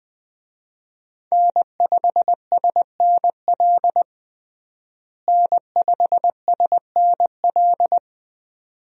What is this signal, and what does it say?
Morse code, heard as a keyed tone, about 21 wpm, 700 Hz. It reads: N5SNL N5SNL
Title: USA Call Signs - 5 Characters - Repeated 2x - 20wpm